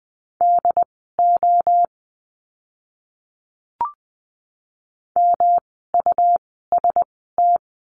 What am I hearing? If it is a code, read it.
Morse code, heard as a keyed tone, about 20 wpm, spaced as for 14 wpm, 700 Hz. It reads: DO MUST
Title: 237 Lesson 38f - Letter q - Words 20wpm